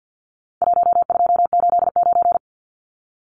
Morse code, heard as a keyed tone, about 50 wpm, 700 Hz. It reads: 1289
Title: Numbers - Contest Style - 4-Digit 50wpm